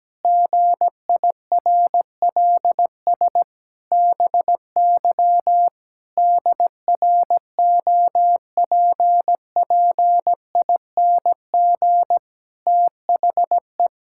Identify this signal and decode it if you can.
Morse code, heard as a keyed tone, about 17 wpm, 700 Hz. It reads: GIRLS BY DROPPING THE
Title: Pangram Challenge v2 - 17wpm